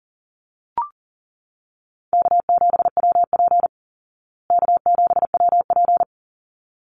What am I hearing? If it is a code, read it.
Morse code, heard as a keyed tone, about 40 wpm, 700 Hz. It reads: K7WP K7WP
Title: Easy Call Signs Repeated 2x - 40wpm